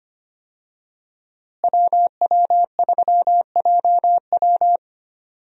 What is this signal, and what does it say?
Morse code, heard as a keyed tone, about 25 wpm, 700 Hz. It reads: WW3JW
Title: USA Call Signs - 5 Characters - Repeated 3x - 25wpm